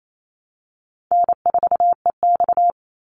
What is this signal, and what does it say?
Morse code, heard as a keyed tone, about 28 wpm, 700 Hz. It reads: N4EX
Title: NA SOTA Chaser Callsigns 28wpm